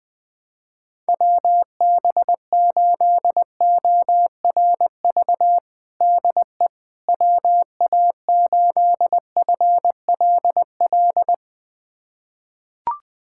Morse code, heard as a keyed tone, about 20 wpm, 700 Hz. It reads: WB8ORV DE WA8FLL
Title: USA Callsign DE Callsign - 6 Characters - 20wpm